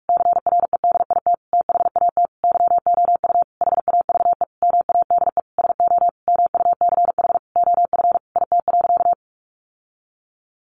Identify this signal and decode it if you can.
Morse code, heard as a keyed tone, about 44 wpm, 700 Hz. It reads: CREDIT THAT YOU HAVE MADE SO MUCH OF IT.
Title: Sentences from Top 1400 Words 45wpm